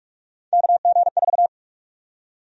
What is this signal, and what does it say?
Morse code, heard as a keyed tone, about 46 wpm, 700 Hz. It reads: KG4
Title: USA Call Sign Prefixes - 45wpm